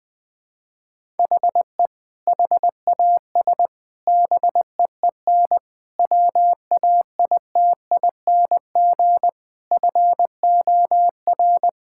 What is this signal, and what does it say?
Morse code, heard as a keyed tone, about 20 wpm, 700 Hz. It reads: HE HAS BEEN WAITING FOR
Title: Sentences with 3-Word Phrases - Spoken Beginning - 20wpm